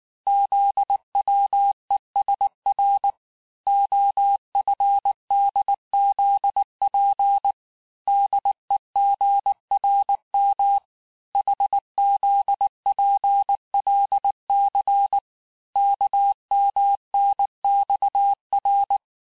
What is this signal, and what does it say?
Morse code, heard as a keyed tone, about 19 wpm, 800 Hz. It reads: ZWESR OFDZP DEGRM HZPLC KMNXR
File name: cours-du-lundi files / cours-1589-texte-chiffre-vitesse-18mpm